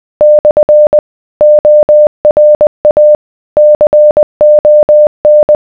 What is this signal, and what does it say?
Morse code, heard as a keyed tone, about 20 wpm, 600 Hz. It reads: / ORA CON